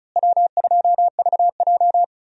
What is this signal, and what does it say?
Morse code, heard as a keyed tone, about 36 wpm, 700 Hz. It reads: W2VJ
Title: USA Call Signs - 4 Characters - Repeated 3x - 35wpm